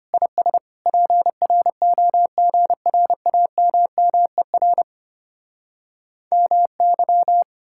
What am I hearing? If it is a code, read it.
Morse code, heard as a keyed tone, about 33 wpm, 700 Hz. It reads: IS PROGRAMMER MY